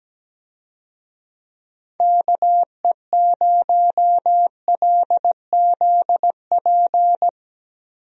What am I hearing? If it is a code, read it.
Morse code, heard as a keyed tone, about 17 wpm, 700 Hz. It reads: KE0LZP
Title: USA Call Signs - 6 Characters - Repeated 3x - 17wpm